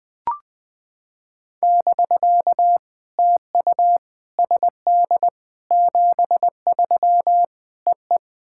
Morse code, heard as a keyed tone, about 20 wpm, 700 Hz. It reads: <BK> TU SD 73 EE
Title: POTA QSO Practice - 20wpm